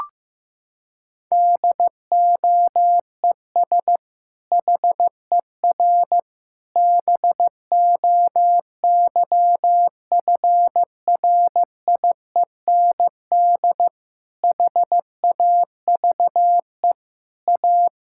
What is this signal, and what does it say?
Morse code, heard as a keyed tone, about 15 wpm, 700 Hz. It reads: DOES HER BOYFRIEND HAVE A
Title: Sentences from Top 1000 Words - Review Part 5 - 15wpm